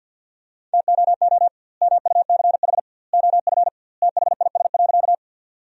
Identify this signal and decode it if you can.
Morse code, heard as a keyed tone, about 51 wpm, 700 Hz. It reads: TOO MUCH OF THIS.
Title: 2x Word-Spacing - Sentences from Top 200 Words - 50wpm